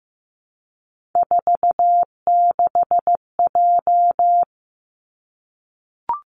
Morse code, heard as a keyed tone, about 15 wpm, 700 Hz. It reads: 46J
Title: Three Challenging Letter-Number Combinations 15wpm